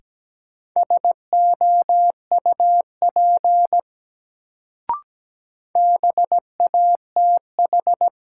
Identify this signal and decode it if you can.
Morse code, heard as a keyed tone, about 17 wpm, 700 Hz. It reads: SOUP BATH
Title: Sentences from Top 1600 Words - New Word Drill Alt - 17wpm